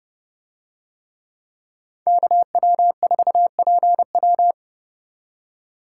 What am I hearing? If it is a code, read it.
Morse code, heard as a keyed tone, about 28 wpm, 700 Hz. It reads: KW4PW